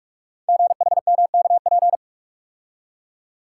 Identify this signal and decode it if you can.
Morse code, heard as a keyed tone, about 44 wpm, 700 Hz. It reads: GHMKP
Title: Code Groups - 5 Letters - 45wpm